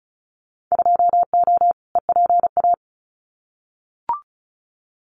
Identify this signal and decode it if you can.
Morse code, heard as a keyed tone, about 34 wpm, 700 Hz. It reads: 2O EPA
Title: Winter Field Day Exchanges Encore 35wpm